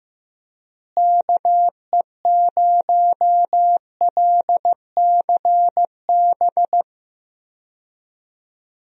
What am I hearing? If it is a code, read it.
Morse code, heard as a keyed tone, about 15 wpm, 700 Hz. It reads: KE0LCB
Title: USA Call Signs - 6 Characters - Repeated 3x - 15wpm